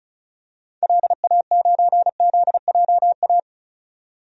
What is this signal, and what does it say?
Morse code, heard as a keyed tone, about 36 wpm, 700 Hz. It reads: LA9ZJA
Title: DXCC-100 Call Signs - 6 Characters - 35wpm